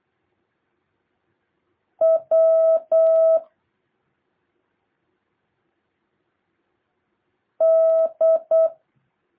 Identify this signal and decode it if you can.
Morse code, heard as a keyed tone, about 8 wpm, 650 Hz. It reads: W D
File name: jh8cbh / 2023cw